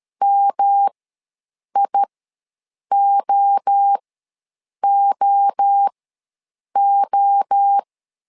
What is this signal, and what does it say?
Morse code, heard as a keyed tone, about 13 wpm, spaced as for 7 wpm, 800 Hz. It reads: MIOOO
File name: LEZIONI / 3 MWRO / 3 MWRO ws 12wpm 1914